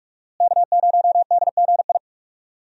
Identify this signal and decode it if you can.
Morse code, heard as a keyed tone, about 41 wpm, 700 Hz. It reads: K0DGI
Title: USA Call Signs - 5 Characters - Repeated 2x - 45wpm